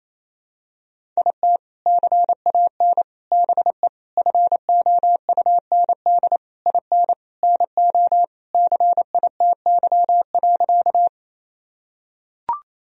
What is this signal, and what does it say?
Morse code, heard as a keyed tone, about 28 wpm, 700 Hz. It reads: IT CAN BE FOUND IN NO CITY.
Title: Sentences from Top 1000 Words - Review Part 1 - 28wpm